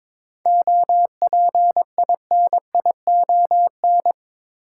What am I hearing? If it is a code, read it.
Morse code, heard as a keyed tone, about 22 wpm, 700 Hz. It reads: OPINION
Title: Sentences from Top 1200 Words - New Word Drill Alt - 22wpm